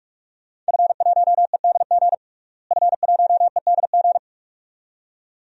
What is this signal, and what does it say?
Morse code, heard as a keyed tone, about 43 wpm, 700 Hz. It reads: F1EDG F1EDG
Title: DXCC-100 Call Signs - 5 Characters - Repeated 2x - 45wpm